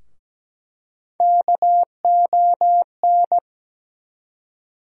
Morse code, heard as a keyed tone, about 17 wpm, 700 Hz. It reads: KON
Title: USA Call Sign Postfixes - 3 Letters 17wpm